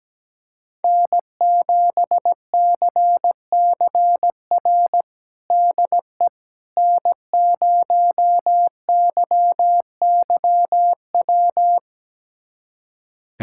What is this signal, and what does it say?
Morse code, heard as a keyed tone, about 17 wpm, 700 Hz. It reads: N7CCR DE N0YYW
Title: USA Callsign DE Callsign - 5 Characters - 17wpm